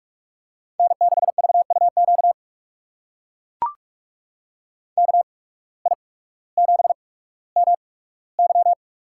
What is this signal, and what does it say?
Morse code, heard as a keyed tone, about 44 wpm, spaced as for 19 wpm, 700 Hz. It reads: N6VUQ KI7MY
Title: USA Call Signs - 8x Character Spacing - 45wpm